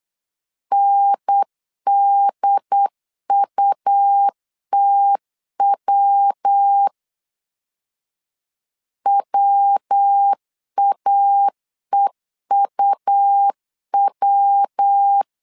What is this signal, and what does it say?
Morse code, heard as a keyed tone, about 8 wpm, 800 Hz. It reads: NDUTW WAEUW
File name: CW / DET / 1 3 1500C S B 8wpm 4141